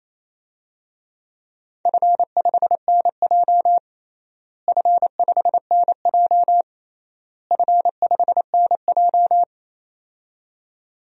Morse code, heard as a keyed tone, about 28 wpm, 700 Hz. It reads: F5NJ F5NJ F5NJ
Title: DXCC-100 Call Signs - 4 Characters - Repeated 3x - 28wpm